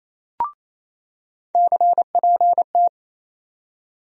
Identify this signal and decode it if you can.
Morse code, heard as a keyed tone, about 28 wpm, 700 Hz. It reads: CPT
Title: USA Call Sign Postfixes - 3 Letters 28wpm